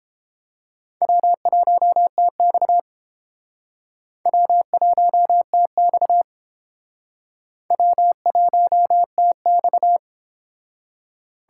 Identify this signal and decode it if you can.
Morse code, heard as a keyed tone, about 29 wpm, 700 Hz. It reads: W1TX W1TX W1TX